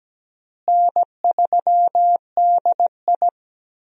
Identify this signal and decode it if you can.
Morse code, heard as a keyed tone, about 17 wpm, 700 Hz. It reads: N3DI
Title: USA Call Signs - 4 Characters - Repeated 3x - 17wpm